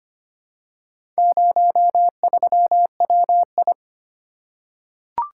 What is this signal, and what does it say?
Morse code, heard as a keyed tone, about 25 wpm, 700 Hz. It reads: 03WI
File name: practice / 25wpm / Four Letter-Number Combinations - 25wpm